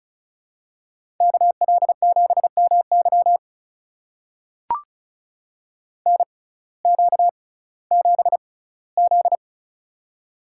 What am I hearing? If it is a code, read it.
Morse code, heard as a keyed tone, about 36 wpm, spaced as for 19 wpm, 700 Hz. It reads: KL7MY NQ7Z
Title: USA Call Signs - 6x Character Spacing - 35wpm